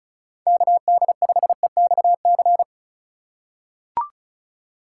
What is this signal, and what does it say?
Morse code, heard as a keyed tone, about 36 wpm, 690 Hz. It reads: KD5EXC
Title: USA Call Signs - 6 Characters - 6x Character Spacing - 35wpm